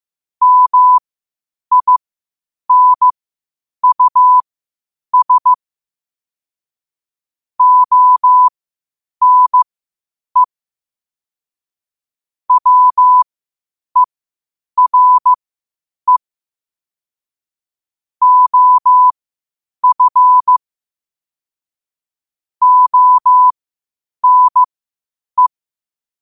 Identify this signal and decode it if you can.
Morse code, heard as a keyed tone, about 15 wpm, spaced as for 8 wpm, 1000 Hz. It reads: MINUS ONE WERE OF ONE
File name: qotdfiles / 8wpm